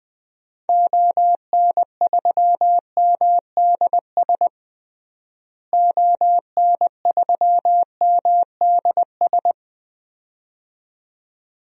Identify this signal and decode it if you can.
Morse code, heard as a keyed tone, about 20 wpm, 700 Hz. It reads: ON3MDS ON3MDS
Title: DXCC-100 Call Signs - 6 Characters - Repeated 2x - 20wpm